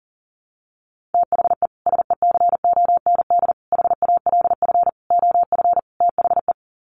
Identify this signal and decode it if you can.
Morse code, heard as a keyed tone, about 40 wpm, 700 Hz. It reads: THE SECOND HALF OF THE
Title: Sets of 5 Words 40wpm